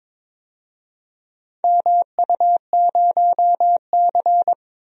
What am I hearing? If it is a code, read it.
Morse code, heard as a keyed tone, about 22 wpm, 700 Hz. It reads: MU0C
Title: DXCC-100 Call Signs - 4 Characters - 22wpm